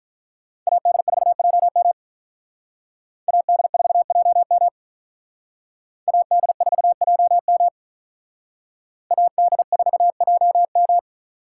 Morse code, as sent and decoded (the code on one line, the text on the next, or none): .- -.. ....- .--- -- / .- -.. ....- .--- -- / .- -.. ....- .--- -- / .- -.. ....- .--- --
AD4JM AD4JM AD4JM AD4JM